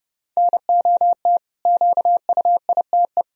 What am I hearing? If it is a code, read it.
Morse code, heard as a keyed tone, about 32 wpm, 700 Hz. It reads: NOT QUITE
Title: Sentences from Top 1000 Words - Review Part 1 - 30wpm